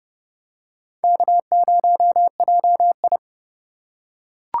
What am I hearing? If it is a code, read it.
Morse code, heard as a keyed tone, about 29 wpm, 700 Hz. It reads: K0JI